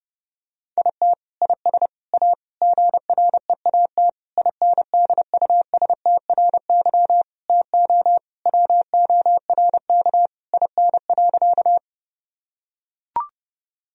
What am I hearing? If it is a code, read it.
Morse code, heard as a keyed tone, about 28 wpm, 700 Hz. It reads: IT IS A GREAT INDUSTRY TO WORK IN.
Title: Sentences from Top 1000 Words - Review Part 4 - 30wpm